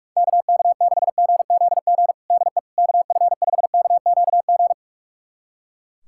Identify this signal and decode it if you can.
Morse code, heard as a keyed tone, about 46 wpm, 700 Hz. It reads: KK6GZG DE KF5KQG
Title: USA Callsign DE Callsign - 6 Characters - 45wpm